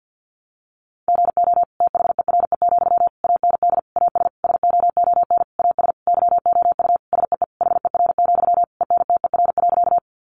Morse code, heard as a keyed tone, about 50 wpm, 700 Hz. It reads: GO THERE, AND AS SOON AS YOU SEE HER, ENTER.
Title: Sentences from Top 1300 Words 50wpm